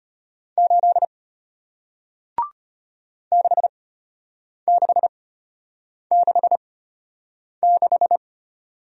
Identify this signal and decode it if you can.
Morse code, heard as a keyed tone, about 33 wpm, spaced as for 8 wpm, 700 Hz. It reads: 8 6666